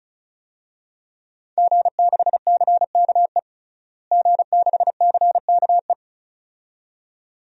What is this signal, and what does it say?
Morse code, heard as a keyed tone, about 35 wpm, 700 Hz. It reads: G6CKE G6CKE
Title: DXCC-100 Call Signs - 5 Characters - Repeated 2x - 35wpm